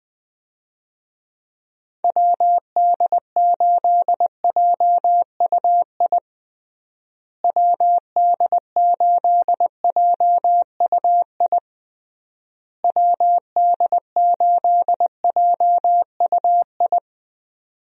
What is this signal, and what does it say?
Morse code, heard as a keyed tone, about 20 wpm, 700 Hz. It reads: WD8JUI WD8JUI WD8JUI
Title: USA Call Signs - 6 Characters - Repeated 3x - 20wpm